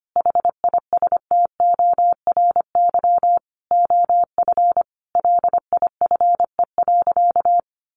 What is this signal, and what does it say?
Morse code, heard as a keyed tone, about 25 wpm, 700 Hz. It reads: HISTORY OF LIFE.
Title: A Sound Of Thunder 25wpm